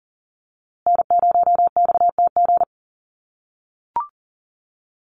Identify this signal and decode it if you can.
Morse code, heard as a keyed tone, about 40 wpm, 700 Hz. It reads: N0XTG
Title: USA Call Signs Repeated 3x 40wpm